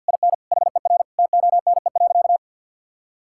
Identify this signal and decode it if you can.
Morse code, heard as a keyed tone, about 50 wpm, 700 Hz. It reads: IN HER TONE.
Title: Sentences from Top 1600 Words 50wpm